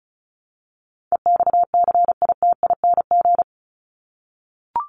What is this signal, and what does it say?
Morse code, heard as a keyed tone, about 34 wpm, 700 Hz. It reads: EXCITING
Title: Top 1751-2000 Words 35wpm